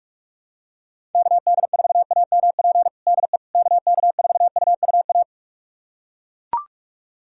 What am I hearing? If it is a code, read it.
Morse code, heard as a keyed tone, about 41 wpm, 700 Hz. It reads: KD4AMP DE KK4UUA
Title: USA Callsign DE Callsign 45wpm